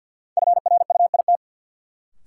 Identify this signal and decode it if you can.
Morse code, heard as a keyed tone, about 47 wpm, 700 Hz. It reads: FRUIT
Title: Sentences from Top 1300 Words - New Word Drill Alt - 50wpm